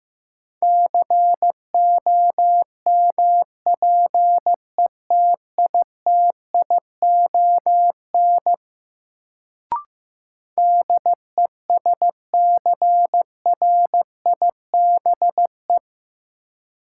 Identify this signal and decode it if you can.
Morse code, heard as a keyed tone, about 15 wpm, 700 Hz. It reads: COMPETITION DESCRIBE
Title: Sentences from Top 1000 Words - New Word Drill - 15wpm